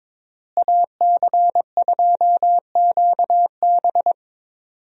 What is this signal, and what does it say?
Morse code, heard as a keyed tone, about 22 wpm, 700 Hz. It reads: AC2QB
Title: USA Call Signs - 5 Characters - Repeated 2x - 22wpm